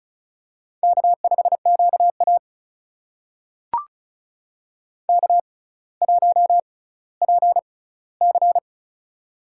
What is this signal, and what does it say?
Morse code, heard as a keyed tone, about 36 wpm, spaced as for 18 wpm, 700 Hz. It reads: K5QA K1PC
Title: USA Call Signs - 4 Characters - 6x Character Spacing - 35wpm